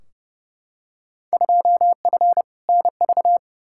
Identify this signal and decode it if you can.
Morse code, heard as a keyed tone, about 30 wpm, 700 Hz. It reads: 2F NV
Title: Field Day Exchanges - 30wpm